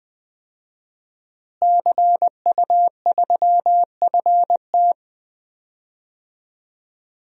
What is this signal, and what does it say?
Morse code, heard as a keyed tone, about 20 wpm, 700 Hz. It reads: CU3FT
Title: DXCC-100 Call Signs - 5 Characters - Repeated 3x - 20wpm